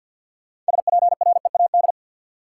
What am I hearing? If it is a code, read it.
Morse code, heard as a keyed tone, about 46 wpm, 700 Hz. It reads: SPREAD
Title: Sentences from Top 1200 Words - New Word Drill Alt - 50wpm